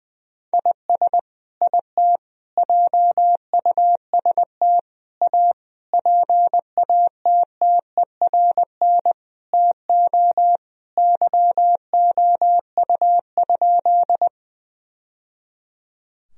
Morse code, heard as a keyed tone, about 20 wpm, 700 Hz. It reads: IS IT JUST A PATTERN TO YOU?
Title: Sentences from Top 1000 Words - Review Part 2 - 20wpm